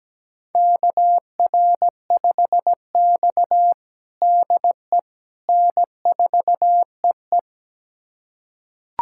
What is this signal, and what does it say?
Morse code, heard as a keyed tone, about 17 wpm, 700 Hz. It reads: KR5X DE N4EE
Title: USA Callsign DE Callsign - 4 Characters - 17wpm